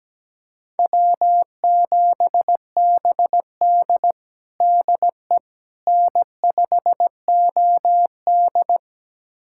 Morse code, heard as a keyed tone, about 17 wpm, 700 Hz. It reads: W7BD DE N5OD
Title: USA Callsign DE Callsign - 4 Characters - 17wpm